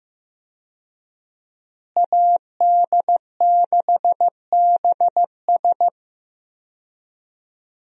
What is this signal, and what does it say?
Morse code, heard as a keyed tone, about 15 wpm, 700 Hz. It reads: AD6BS
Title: USA Call Signs - 5 Characters - Repeated 3x - 15wpm